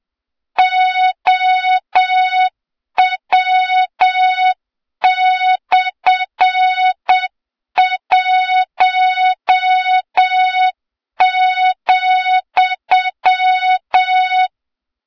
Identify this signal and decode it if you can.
Morse code, heard as a keyed tone, about 8 wpm, 3700 Hz, with noise. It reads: OW/1,
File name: Morze-HA2MN / hang / 23 GYAKORLAT